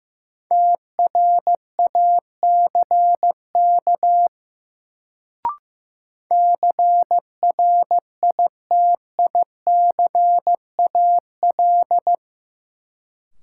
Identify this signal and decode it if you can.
Morse code, heard as a keyed tone, about 15 wpm, 700 Hz. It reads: TRACK CRITICAL
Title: Sentences from Top 1100 Words - New Word Drill Alt - 15wpm